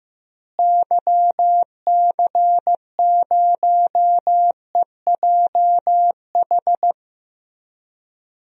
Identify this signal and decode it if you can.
Morse code, heard as a keyed tone, about 15 wpm, 700 Hz. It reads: YC0EJH
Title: DXCC-100 Call Signs - 6 Characters - Repeated 3x - 15wpm